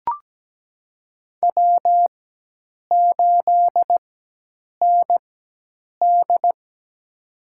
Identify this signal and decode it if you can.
Morse code, heard as a keyed tone, about 17 wpm, spaced as for 8 wpm, 700 Hz. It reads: W8ND
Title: USA Call Signs - 4 Characters - 4x Character Spacing - 17wpm